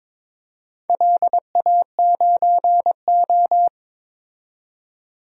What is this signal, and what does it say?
Morse code, heard as a keyed tone, about 22 wpm, 700 Hz. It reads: LA9O
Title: DXCC-100 Call Signs - 4 Characters - 22wpm